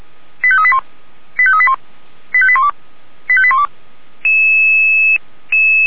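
Five-tone selective call (ZVEI), heard as quick runs of five tones, tones of about 70 ms. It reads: 85381 85381 85812 85812
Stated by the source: ZVEI five-tone selective call received by SDR, from audio.f85360.feuerwehr-orlamuende.org fire brigade server.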